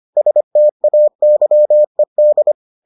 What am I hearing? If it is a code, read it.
Morse code, heard as a keyed tone, about 25 wpm, 600 Hz. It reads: STAYED